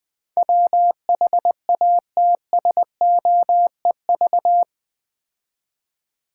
Morse code, heard as a keyed tone, about 20 wpm, 700 Hz. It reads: WHATSOEV
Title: Build A Word - 4 Letters At A Time - 10-Letter Words - 20wpm